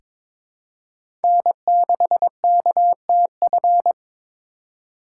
Morse code, heard as a keyed tone, about 22 wpm, 700 Hz. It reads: N6KTF
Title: USA Call Signs - 5 Characters - 3x Character Spacing - 22wpm